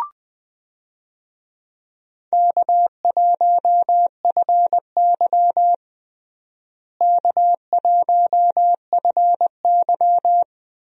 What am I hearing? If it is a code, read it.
Morse code, heard as a keyed tone, about 21 wpm, 700 Hz. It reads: K1FY K1FY
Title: USA Call Signs - 4 Characters - Repeated 2x - 20wpm